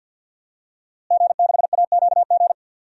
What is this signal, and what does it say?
Morse code, heard as a keyed tone, about 51 wpm, 700 Hz. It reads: G6AQG
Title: DXCC-100 Call Signs - 5 Characters - Repeated 3x - 50wpm